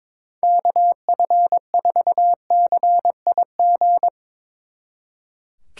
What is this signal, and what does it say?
Morse code, heard as a keyed tone, about 24 wpm, 700 Hz. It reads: KF4CIG
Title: Speed Racing - USA 2x3 Call Signs 22wpm